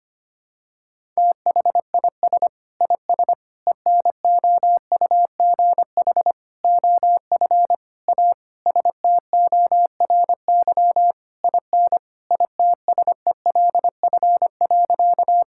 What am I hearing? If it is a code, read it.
Morse code, heard as a keyed tone, about 25 wpm, 700 Hz. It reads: THIS IS ENOUGH OF A STORY IN ITSELF.
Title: Sentences from Top 1000 Words - Review Part 5 - 25wpm